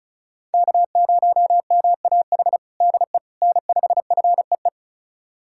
Morse code, heard as a keyed tone, about 34 wpm, 700 Hz. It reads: K0MAH DE N5FEE
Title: USA Callsign DE Callsign 35wpm